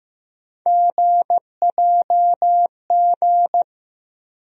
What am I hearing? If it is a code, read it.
Morse code, heard as a keyed tone, about 15 wpm, 700 Hz. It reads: GJG
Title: USA Call Sign Postfixes - 3 Letters 15wpm